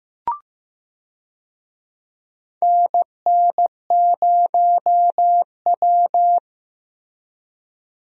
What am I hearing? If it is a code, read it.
Morse code, heard as a keyed tone, about 15 wpm, 700 Hz. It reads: NN0W
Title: USA Call Signs - 4 Characters - Repeated 3x - 15wpm